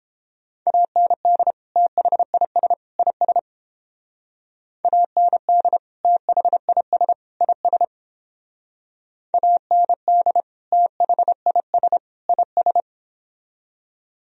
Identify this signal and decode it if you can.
Morse code, heard as a keyed tone, about 29 wpm, 700 Hz. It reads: AND THIS IS AND THIS IS AND THIS IS